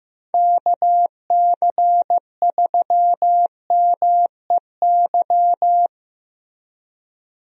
Morse code, heard as a keyed tone, about 15 wpm, 700 Hz. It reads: KC3MEY
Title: USA Call Signs - 6 Characters - Repeated 3x - 15wpm